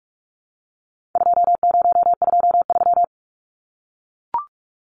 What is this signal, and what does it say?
Morse code, heard as a keyed tone, about 44 wpm, 700 Hz. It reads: 2023
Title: Numbers - 4-Digit 45wpm